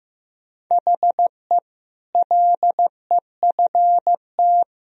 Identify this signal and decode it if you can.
Morse code, heard as a keyed tone, about 15 wpm, 700 Hz. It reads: HE LEFT
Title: Sentences from Top 500 Words and Abbreviations 15wpm